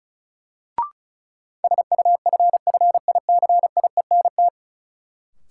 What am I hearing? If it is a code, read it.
Morse code, heard as a keyed tone, about 36 wpm, 700 Hz. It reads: SUFFICIENT
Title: Sentences from Top 1400 Words - New Word Drill Alt - 35wpm